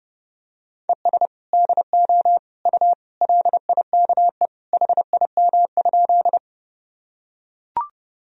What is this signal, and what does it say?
Morse code, heard as a keyed tone, about 30 wpm, 700 Hz. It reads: ES DO U LIKE HIM?
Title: Sentences from Top 100 Words and Abbreviations 30wpm